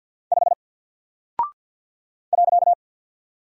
Morse code, heard as a keyed tone, about 48 wpm, 700 Hz. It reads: 5 .
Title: Single Character (Letter:Number:Punctuation:Prosign) Alt 50wpm